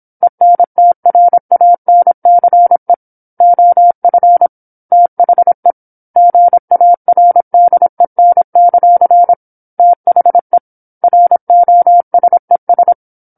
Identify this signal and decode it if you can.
Morse code, heard as a keyed tone, about 27 wpm, 700 Hz. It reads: ENTRANCE OF THE GARDEN; THE ROSES
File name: 08-Chapter-08 26 WPM